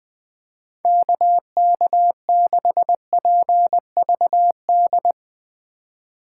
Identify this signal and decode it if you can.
Morse code, heard as a keyed tone, about 21 wpm, 700 Hz. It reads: KK6PVD
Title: USA Call Signs - 6 Characters - 3x Character Spacing - 20wpm